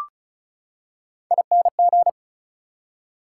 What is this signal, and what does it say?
Morse code, heard as a keyed tone, about 34 wpm, 700 Hz. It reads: ING
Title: Prefixes and Suffixes 35wpm